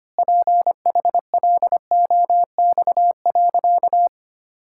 Morse code, heard as a keyed tone, about 25 wpm, 700 Hz. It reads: PHLOX.
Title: Pangram Challenge v2 - 25wpm